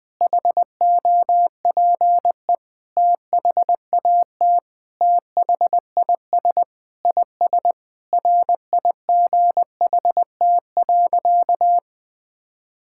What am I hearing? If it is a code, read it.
Morse code, heard as a keyed tone, about 20 wpm, 700 Hz. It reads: HOPE THAT THIS IS RIGHT.
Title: Sentences with 3-Word Phrases - Spoken Beginning - 20wpm